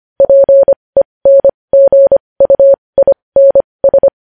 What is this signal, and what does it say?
Morse code, heard as a keyed tone, about 25 wpm, spaced as for 20 wpm, 550 Hz. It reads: PENGUINS
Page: news headlines in morse code at 20 wpm